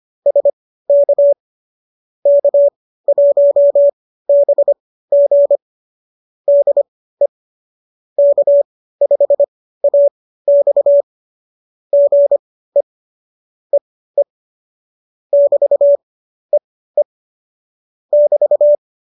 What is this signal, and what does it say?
Morse code, heard as a keyed tone, about 25 wpm, spaced as for 15 wpm, 570 Hz. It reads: SK K1BG DE K5AX GE E E <BT> E E <BT>